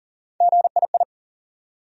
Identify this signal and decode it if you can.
Morse code, heard as a keyed tone, about 40 wpm, 700 Hz. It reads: GII
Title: USA Call Sign Postfixes - 3 Letters 40wpm